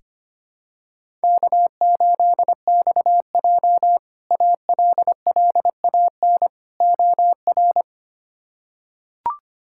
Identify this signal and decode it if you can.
Morse code, heard as a keyed tone, about 25 wpm, 700 Hz. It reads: K8XJ ALLAN OR
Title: K1USN SST QSO Practice - 25wpm